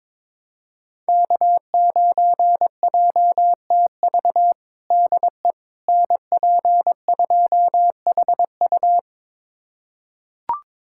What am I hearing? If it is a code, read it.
Morse code, heard as a keyed tone, about 22 wpm, 700 Hz. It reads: K9JTV DE NP2HU
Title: USA Callsign DE Callsign - 5 Characters - 22wpm